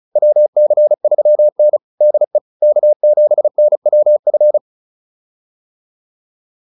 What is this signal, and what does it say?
Morse code, heard as a keyed tone, about 34 wpm, 600 Hz. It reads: WC3N DE K7NWF